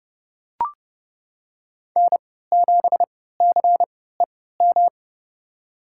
Morse code, heard as a keyed tone, about 30 wpm, spaced as for 17 wpm, 700 Hz. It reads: N7CEM
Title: USA Call Signs - 5 Characters - 3x Character Spacing - 30wpm